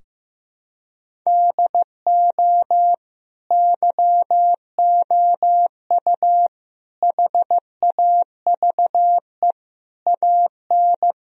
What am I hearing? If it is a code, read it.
Morse code, heard as a keyed tone, about 15 wpm, 700 Hz. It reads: DO YOU HAVE AN
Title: Sentences from Top 1000 Words - Review Part 3 - 15wpm